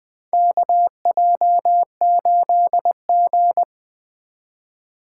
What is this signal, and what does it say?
Morse code, heard as a keyed tone, about 20 wpm, 700 Hz. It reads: KJ8G
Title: USA Call Signs - 4 Characters - 2x Character Spacing - 20wpm